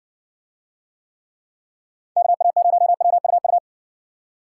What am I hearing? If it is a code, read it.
Morse code, heard as a keyed tone, about 55 wpm, 700 Hz. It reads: KA0WUU